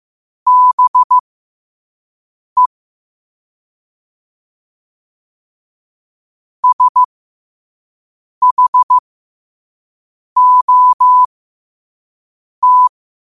Morse code, heard as a keyed tone, about 15 wpm, spaced as for 5 wpm, 1000 Hz. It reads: BE SHOT
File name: qotdfiles / 5wpm